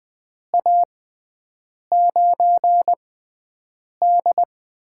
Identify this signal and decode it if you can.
Morse code, heard as a keyed tone, about 20 wpm, spaced as for 7 wpm, 700 Hz. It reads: A9D
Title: USA Call Signs - 6x Character Spacing - 20wpm